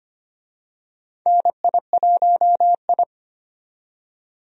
Morse code, heard as a keyed tone, about 25 wpm, 700 Hz. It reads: NI1I
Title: USA Call Signs - 4 Characters - 25wpm